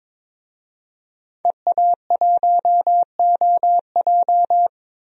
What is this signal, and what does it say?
Morse code, heard as a keyed tone, about 22 wpm, 700 Hz. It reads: EA1OJ
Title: DXCC-100 Call Signs - 5 Characters - 22wpm